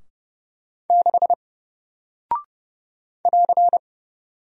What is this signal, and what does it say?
Morse code, heard as a keyed tone, about 30 wpm, 700 Hz. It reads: 6 <AR>
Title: Single Character (Letter:Number:Punctuation:Prosign) Alt 30wpm